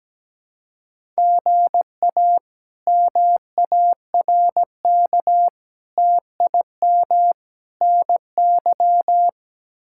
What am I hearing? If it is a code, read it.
Morse code, heard as a keyed tone, about 17 wpm, 700 Hz. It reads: GA MARK TIM NY
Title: K1USN SST QSO Practice - 17wpm